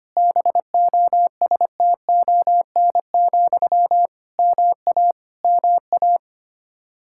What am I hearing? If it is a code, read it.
Morse code, heard as a keyed tone, about 25 wpm, 700 Hz. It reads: BOSTON, MA MA
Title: Practice QSO - Exchange 1 - Beginning - 25wpm